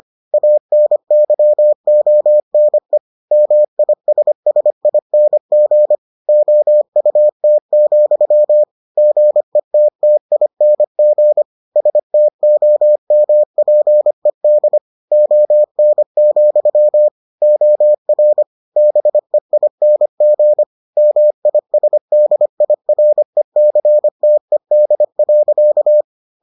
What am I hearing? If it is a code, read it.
Morse code, heard as a keyed tone, about 25 wpm, 600 Hz. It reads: ANYONE MISSING OUT, GETTING STOMPED ON, OR BEING MISDIRECTED.